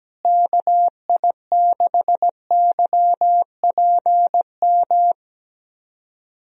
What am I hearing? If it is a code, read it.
Morse code, heard as a keyed tone, about 17 wpm, 700 Hz. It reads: KI6YPM
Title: USA Call Signs - 6 Characters - Repeated 3x - 17wpm